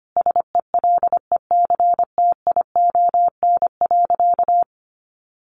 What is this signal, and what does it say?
Morse code, heard as a keyed tone, about 25 wpm, 700 Hz. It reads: SELECTION.
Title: Sentences from Top 1400 Words 25wpm